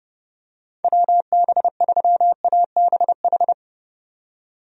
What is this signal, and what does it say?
Morse code, heard as a keyed tone, about 29 wpm, 700 Hz. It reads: WB3ABH
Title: USA Call Signs - 6 Characters - 30wpm